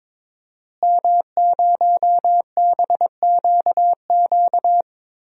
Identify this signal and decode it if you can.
Morse code, heard as a keyed tone, about 22 wpm, 700 Hz. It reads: M0BQQ
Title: DXCC-100 Call Signs - 5 Characters - Repeated 2x - 22wpm